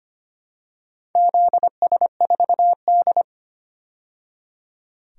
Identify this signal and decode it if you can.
Morse code, heard as a keyed tone, about 25 wpm, 700 Hz. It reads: ZS4D
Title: DXCC-100 Call Signs - 4 Characters - 25wpm